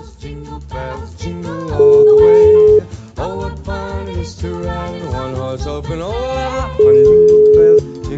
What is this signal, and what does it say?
Ringback tone (Europe, 425 Hz).